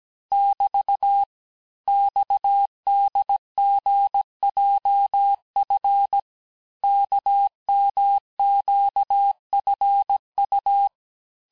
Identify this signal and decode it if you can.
Morse code, heard as a keyed tone, about 17 wpm, 800 Hz. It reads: <BT> XDGJF KMQFU
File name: cours-du-lundi files / corrige-cours-1581-texte-chiffre-vitesse-16-mpm